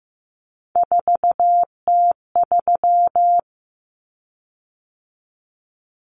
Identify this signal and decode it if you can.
Morse code, heard as a keyed tone, about 15 wpm, 700 Hz. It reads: 4T3
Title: Common Cut Numbers - 3-Digit 15wpm